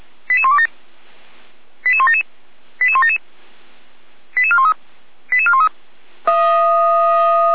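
Five-tone selective call (ZVEI), heard as quick runs of five tones, tones of about 70 ms. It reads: 88127 88177 88177 88413 88413
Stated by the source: ZVEI five-tone selective call received by SDR.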